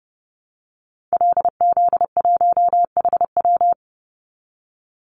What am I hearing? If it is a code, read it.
Morse code, heard as a keyed tone, about 30 wpm, 700 Hz. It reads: LZ1HW
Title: Call Signs Repeated 2x - 30wpm